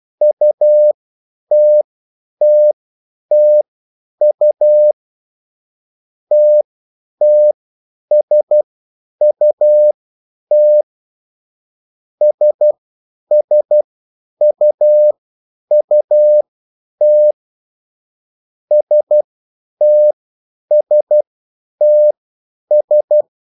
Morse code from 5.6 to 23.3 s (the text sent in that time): TTSUT SSUUT STSTS